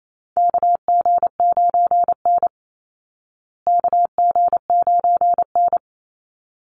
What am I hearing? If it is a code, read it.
Morse code, heard as a keyed tone, about 28 wpm, 700 Hz. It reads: KG9N KG9N
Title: Call Signs Repeated 3x - 28wpm